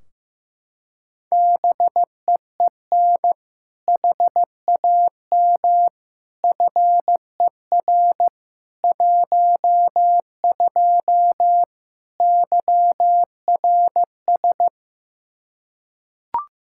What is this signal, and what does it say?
Morse code, heard as a keyed tone, about 15 wpm, 700 Hz. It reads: BEEN HAM FER 12 YRS
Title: Practice QSO - Exchange 3 - More Info - 15wpm